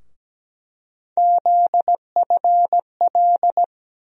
Morse code, heard as a keyed tone, about 17 wpm, 700 Hz. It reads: ZFL
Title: USA Call Sign Postfixes - 3 Letters 17wpm